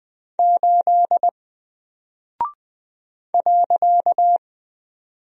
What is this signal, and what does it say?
Morse code, heard as a keyed tone, about 20 wpm, 700 Hz. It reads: 8 .
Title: Single Character (Letter:Number:Punctuation:Prosign) 20wpm